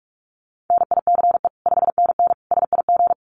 Morse code, heard as a keyed tone, about 44 wpm, 700 Hz. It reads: NICE 5NN SIG
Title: Common QSO Phrases 45wpm